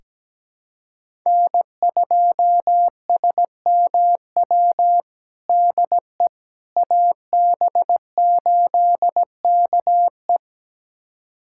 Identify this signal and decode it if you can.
Morse code, heard as a keyed tone, about 17 wpm, 700 Hz. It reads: N2SMW DE AB8KE
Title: USA Callsign DE Callsign - 5 Characters - 17wpm